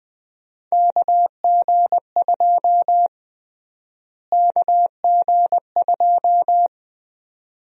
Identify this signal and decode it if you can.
Morse code, heard as a keyed tone, about 20 wpm, 700 Hz. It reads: KG2 KG2
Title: USA Call Sign Prefixes - Repeated 2x - 20wpm